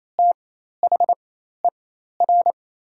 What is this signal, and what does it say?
Morse code, heard as a keyed tone, about 28 wpm, spaced as for 13 wpm, 700 Hz. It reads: THER
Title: 4x Character Spacing - Top 300 Words - 28wpm